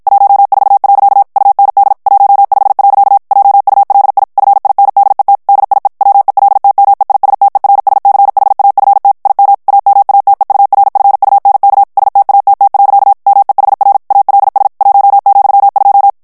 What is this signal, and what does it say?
Morse code, heard as a keyed tone, about 55 wpm, 800 Hz. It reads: 14. UND 15. JULI FINDET DIE WELTMEISTERSCHAFT IM AMATEURFUNK STATT. MEHR ALS 1,2
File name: morse mp3 / ALLEMAND texte clair 2 vitesse 50 mots minute